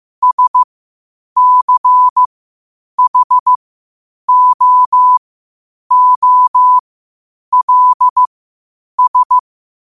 Morse code, heard as a keyed tone, about 15 wpm, spaced as for 8 wpm, 1000 Hz. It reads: SCHOOLS
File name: qotdfiles / 8wpm